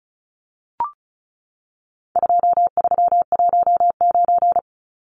Morse code, heard as a keyed tone, about 34 wpm, 700 Hz. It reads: 2319
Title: Numbers - 4-Digit 35wpm